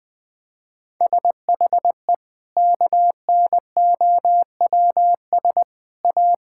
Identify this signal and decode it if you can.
Morse code, heard as a keyed tone, about 20 wpm, 700 Hz. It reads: SHE KNOWS A
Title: Sentences with 4-Word Phrases - 20wpm